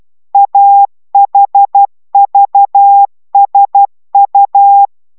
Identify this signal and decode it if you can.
Morse code, heard as a keyed tone, about 12 wpm, 800 Hz. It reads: AHVSU
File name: taviro tanulas / cw 69 04 10